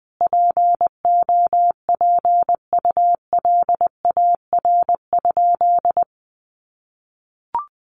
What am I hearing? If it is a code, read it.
Morse code, heard as a keyed tone, about 20 wpm, 700 Hz. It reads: POPULAR?
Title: Sentences from Top 1200 Words 20wpm